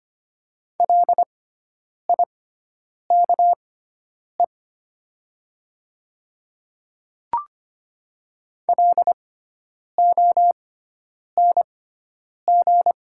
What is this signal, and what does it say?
Morse code, heard as a keyed tone, about 25 wpm, spaced as for 9 wpm, 690 Hz. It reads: LIKE LONG
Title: 6x Character Spacing - Top 200 Words - 25wpm